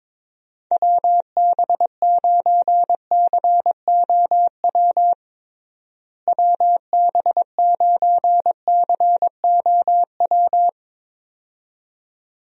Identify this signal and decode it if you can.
Morse code, heard as a keyed tone, about 23 wpm, 700 Hz. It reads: WB9COW WB9COW
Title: USA Call Signs - 6 Characters - Repeated 2x - 22wpm